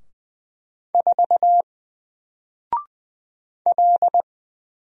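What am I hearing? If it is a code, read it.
Morse code, heard as a keyed tone, about 20 wpm, 700 Hz. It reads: 4 L
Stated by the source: Single letter and number Morse drill at 20 wpm, from N4LCD.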